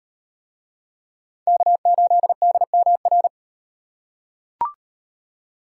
Morse code, heard as a keyed tone, about 36 wpm, 700 Hz. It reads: K8DMR